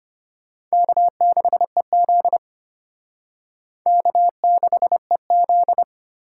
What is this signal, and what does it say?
Morse code, heard as a keyed tone, about 31 wpm, 700 Hz. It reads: K6EZ K6EZ
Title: Speed Racing - USA 1x2 Call Signs 25wpm